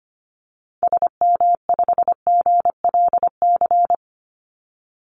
Morse code, heard as a keyed tone, about 25 wpm, 700 Hz. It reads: SM5GLC
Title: Call Signs - 25wpm